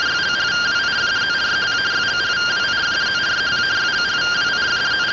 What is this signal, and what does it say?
Two-tone teleprinter signal, about 170 Hz shift.